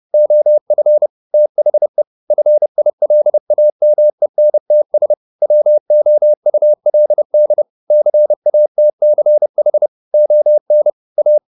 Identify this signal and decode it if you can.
Morse code, heard as a keyed tone, about 30 wpm, 600 Hz. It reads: OF THE FILAMENTS WOULD CATCH ON A
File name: the old man and the sea / the old man and the sea 1 30wpm